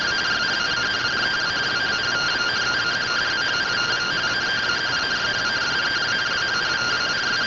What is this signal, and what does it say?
Two-tone teleprinter signal, about 170 Hz shift.
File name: findings / SAGITAL fr15 nb203 3mm.gen-32 rtty rx